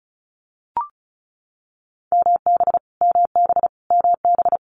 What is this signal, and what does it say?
Morse code, heard as a keyed tone, about 36 wpm, 700 Hz. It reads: MB MB MB
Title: Multi-letter Phonemes Repeated 3x 35wpm